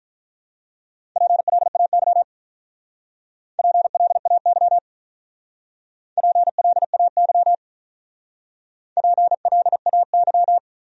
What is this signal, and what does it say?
Morse code, heard as a keyed tone, about 44 wpm, 700 Hz. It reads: PLAY PLAY PLAY PLAY